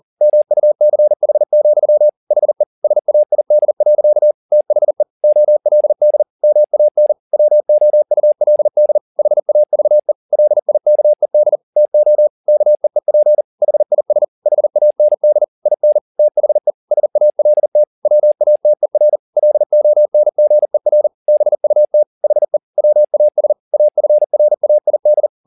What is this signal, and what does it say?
Morse code, heard as a keyed tone, about 40 wpm, 600 Hz. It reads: MUCH, HE SAID. THE OLD MAN WOULD HAVE LIKED TO KEEP HIS HAND IN THE SALT WATER LONGER BUT HE WAS AFRAID